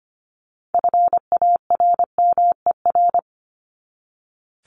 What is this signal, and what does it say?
Morse code, heard as a keyed tone, about 25 wpm, 700 Hz. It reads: FARMER
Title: Top 1751-2000 Words 25wpm